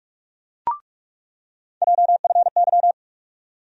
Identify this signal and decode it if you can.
Morse code, heard as a keyed tone, about 46 wpm, 700 Hz. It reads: JFY
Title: USA Call Sign Postfixes - 3 Letters 45wpm